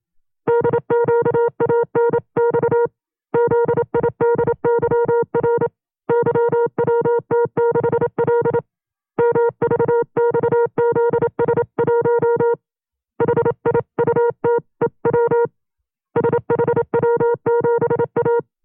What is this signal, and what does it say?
Morse code, heard as a keyed tone, about 27 wpm, 470 Hz. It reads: DQANX ZIDYR YWT6L MVXZS1 HIUTEW SHW7A
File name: FreeFiles / CW-practice / 25WPM-side1